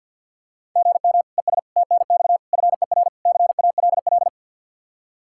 Morse code, heard as a keyed tone, about 51 wpm, 700 Hz. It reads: GM ES TNX FER CALL
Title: K1USN SST QSO Practice 50wpm